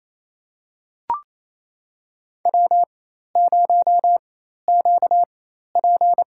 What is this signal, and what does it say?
Morse code, heard as a keyed tone, about 28 wpm, spaced as for 13 wpm, 700 Hz. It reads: W0QP
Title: USA Call Signs - 4 Characters - 4x Character Spacing - 28wpm